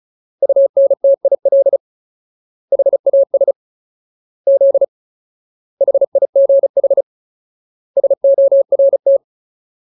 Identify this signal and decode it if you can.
Morse code, heard as a keyed tone, about 34 wpm, 550 Hz. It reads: UNTIL HAS Z HIGH SORT
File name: Ultimate Word List / 35WPM Ultimate